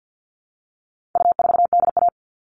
Morse code, heard as a keyed tone, about 50 wpm, 700 Hz. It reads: U4DA
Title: All Cut Numbers - 4-Digit 50wpm